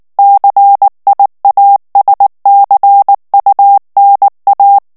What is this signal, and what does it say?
Morse code, heard as a keyed tone, about 19 wpm, 800 Hz. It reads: CIASCUNA
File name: morse mp3 / ITALIEN texte clair 2 vitesse 18 mots minute